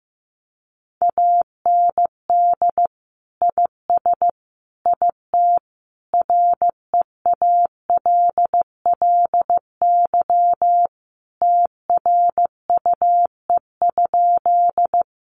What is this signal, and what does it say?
Morse code, heard as a keyed tone, about 15 wpm, 700 Hz. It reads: AND IS IT REALLY TRUE?
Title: Sentences from Top 600 Words 15wpm